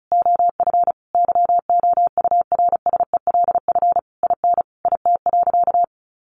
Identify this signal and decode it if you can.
Morse code, heard as a keyed tone, about 34 wpm, 700 Hz. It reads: OF YOURSELF IN IT.
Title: Sentences from Top 1300 Words 35wpm